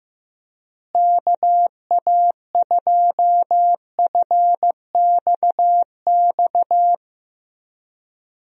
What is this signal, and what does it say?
Morse code, heard as a keyed tone, about 15 wpm, 700 Hz. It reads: KA2FXX
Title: USA Call Signs - 6 Characters - Repeated 3x - 15wpm